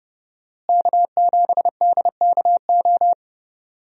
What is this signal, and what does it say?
Morse code, heard as a keyed tone, about 32 wpm, 700 Hz. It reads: K7DKO
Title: USA Call Signs - 5 Characters - Repeated 2x - 30wpm